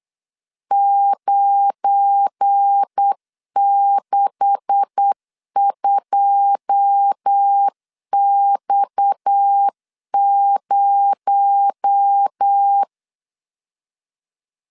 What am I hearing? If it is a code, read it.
Morse code, heard as a keyed tone, about 8 wpm, 800 Hz. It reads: 962X0
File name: LEZIONI / 9 286 8wpm 3425